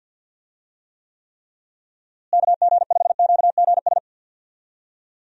Morse code, heard as a keyed tone, about 50 wpm, 700 Hz. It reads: KG5QGS
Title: USA Call Signs - 6 Characters - Repeated 3x - 50wpm